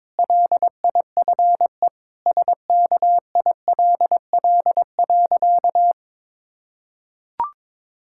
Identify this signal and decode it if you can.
Morse code, heard as a keyed tone, about 22 wpm, 700 Hz. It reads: LIFE SKILL.
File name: Difficult Sentences with Hints - 22wpm